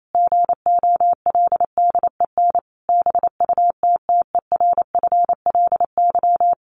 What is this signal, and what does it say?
Morse code, heard as a keyed tone, about 28 wpm, 700 Hz. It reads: GOLDEN BUTTERFLY
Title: A Sound Of Thunder 28wpm